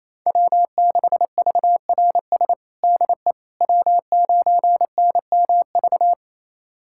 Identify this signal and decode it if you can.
Morse code, heard as a keyed tone, about 28 wpm, 700 Hz. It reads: W6VRS DE W9NMV
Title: USA Callsign DE Callsign - 5 Characters - 28wpm